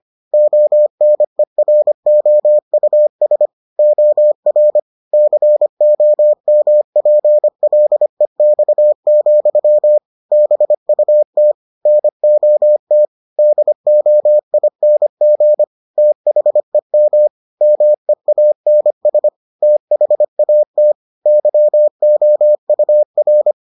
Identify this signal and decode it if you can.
Morse code, heard as a keyed tone, about 25 wpm, 600 Hz. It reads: ONEROUS OR COMPLEX, BUT NOT DOING THEM MEANS THAT YOUR